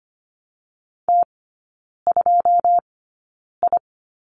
Morse code, heard as a keyed tone, about 25 wpm, spaced as for 9 wpm, 700 Hz. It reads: T2I
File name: course / 25:10wpm / 093 Lesson 16e - Letter c - Three Characters 25wpm